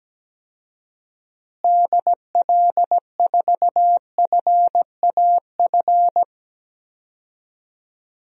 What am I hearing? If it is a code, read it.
Morse code, heard as a keyed tone, about 17 wpm, 700 Hz. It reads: DL4FAF
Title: DXCC-100 Call Signs - 6 Characters - Repeated 3x - 17wpm